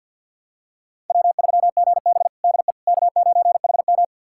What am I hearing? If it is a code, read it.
Morse code, heard as a keyed tone, about 46 wpm, 700 Hz. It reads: W3GB DE K9HM
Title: USA Callsign DE Callsign - 4 Characters - 50wpm